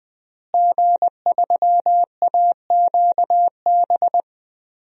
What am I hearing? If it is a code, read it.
Morse code, heard as a keyed tone, about 20 wpm, 700 Hz. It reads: G3AQB
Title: DXCC-100 Call Signs - 5 Characters - Repeated 2x - 20wpm